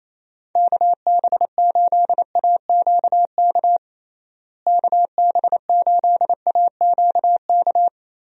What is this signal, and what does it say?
Morse code, heard as a keyed tone, about 27 wpm, 700 Hz. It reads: KB8AQK KB8AQK
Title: USA Call Signs - 6 Characters - Repeated 3x - 28wpm